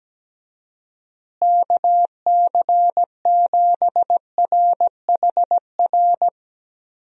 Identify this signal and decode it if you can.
Morse code, heard as a keyed tone, about 17 wpm, 700 Hz. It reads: KC7RHR
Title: USA Call Signs - 6 Characters - Repeated 3x - 17wpm